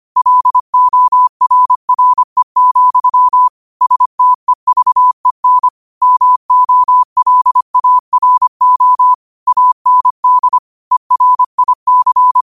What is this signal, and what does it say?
Morse code, heard as a keyed tone, about 25 wpm, 1000 Hz. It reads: LORRE, STEVEN MOLARO AND ERIC
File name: qotdfiles / 25wpm